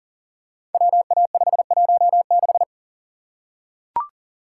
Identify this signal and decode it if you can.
Morse code, heard as a keyed tone, about 38 wpm, 700 Hz. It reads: WA516
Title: Code Groups - 5 Letter:Numbers - 40wpm